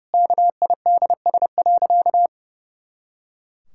Morse code, heard as a keyed tone, about 32 wpm, 700 Hz. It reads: KIDS.
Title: Pangram Challenge v2 - 30wpm